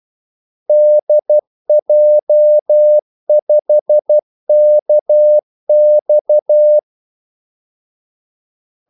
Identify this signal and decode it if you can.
Morse code, heard as a keyed tone, about 12 wpm, 600 Hz. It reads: DJ5KX
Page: 2 ekstra ordmellemrum mellem kaldesignalerne